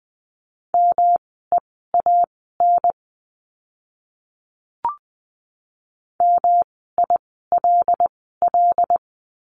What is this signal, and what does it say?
Morse code, heard as a keyed tone, about 20 wpm, spaced as for 14 wpm, 700 Hz. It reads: MEAN MILL
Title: 108 Lesson 18f - Letter m - Words 20wpm